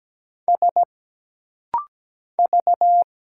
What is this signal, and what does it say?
Morse code, heard as a keyed tone, about 17 wpm, 700 Hz. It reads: S V
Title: Single Letters - Rapid-Fire 17wpm